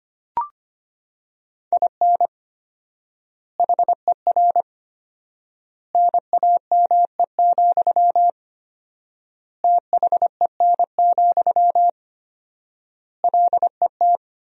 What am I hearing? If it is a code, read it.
Morse code, heard as a keyed tone, about 25 wpm, 700 Hz. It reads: IN HER NAME, THEN, LET
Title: 4x Word-Spacing - Sentences from Top 500 Words - 25wpm